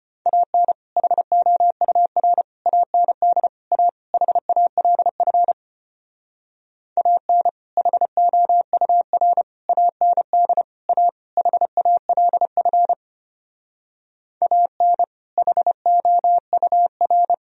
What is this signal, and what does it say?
Morse code, heard as a keyed tone, about 32 wpm, 700 Hz. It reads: AN HOUR AND A HALF AN HOUR AND A HALF AN HOUR